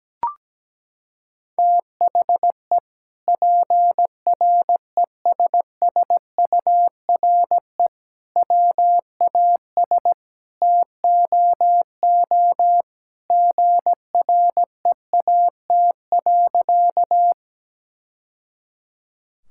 Morse code, heard as a keyed tone, about 17 wpm, 700 Hz. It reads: THE PRESSURE WAS TOO GREAT.
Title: Sentences from Top 1000 Words - Review Part 6 - 17wpm